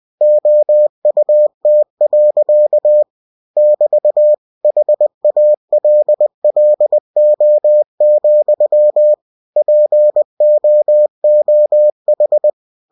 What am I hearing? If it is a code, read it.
Morse code, heard as a keyed tone, about 20 wpm, 600 Hz. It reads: OUT. <BT> HALLO, POOH
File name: cw-books / winnie-the-pooh / winnie-20wpm-page-0076